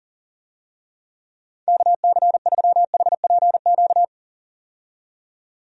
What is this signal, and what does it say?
Morse code, heard as a keyed tone, about 40 wpm, 700 Hz. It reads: KC3HPQ
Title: USA Call Signs - 6 Characters - 40wpm